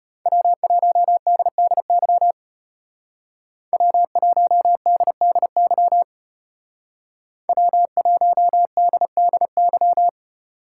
Morse code, heard as a keyed tone, about 35 wpm, 700 Hz. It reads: W1DDY W1DDY W1DDY